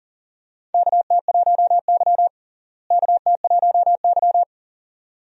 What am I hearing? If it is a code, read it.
Morse code, heard as a keyed tone, about 40 wpm, 700 Hz. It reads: KT1Y KT1Y
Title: USA Call Signs - 4 Characters - Repeated 2x - 40wpm